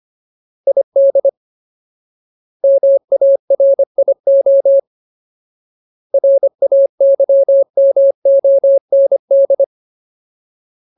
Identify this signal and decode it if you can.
Morse code, heard as a keyed tone, about 25 wpm, 550 Hz. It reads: ID MARIO RAYMOND
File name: Ultimate Word List / 25WPM Ultimate